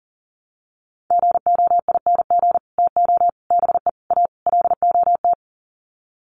Morse code, heard as a keyed tone, about 38 wpm, 700 Hz. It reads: GOING TO BE A LOT
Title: Sets of 5 Words Encore 40wpm